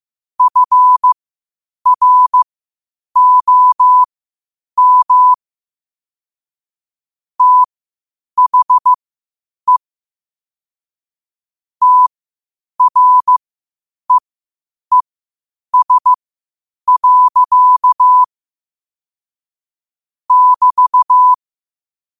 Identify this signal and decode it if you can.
Morse code, heard as a keyed tone, about 15 wpm, spaced as for 8 wpm, 1000 Hz. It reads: FROM THE TREES. <BT>